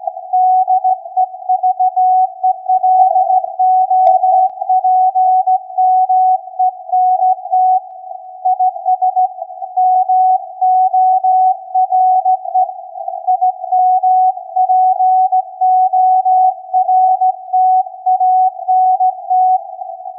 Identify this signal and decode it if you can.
Morse code, heard as a keyed tone, about 16 wpm, 740 Hz. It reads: DEVELOPMENT IS MORE IMPORTANT
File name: iz7slz / VLF / SAQ 16-11-2022